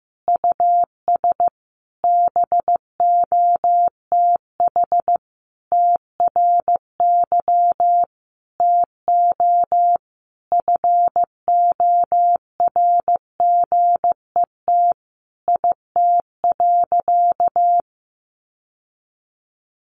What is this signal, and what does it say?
Morse code, heard as a keyed tone, about 15 wpm, 700 Hz. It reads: US BOTH TRY TO FORGET IT.
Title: Sentences from Top 1200 Words 15wpm